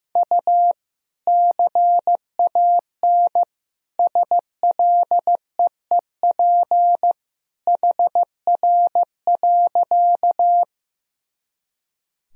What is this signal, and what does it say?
Morse code, heard as a keyed tone, about 15 wpm, 700 Hz. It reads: U CAN SLEEP HR.
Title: Sentences from Top 500 Words and Abbreviations 15wpm